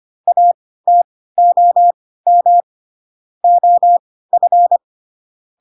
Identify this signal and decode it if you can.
Morse code, heard as a keyed tone, about 25 wpm, spaced as for 16 wpm, 700 Hz. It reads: ATOM OF